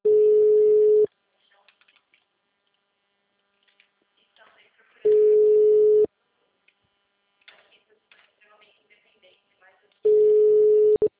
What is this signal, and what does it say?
Ringback tone (Europe, 425 Hz), with speech.
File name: out- -2035- - - .19